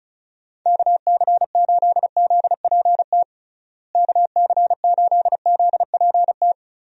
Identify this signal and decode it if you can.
Morse code, heard as a keyed tone, about 36 wpm, 700 Hz. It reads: KC8ZPT KC8ZPT
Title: USA Call Signs - 6 Characters - Repeated 3x - 35wpm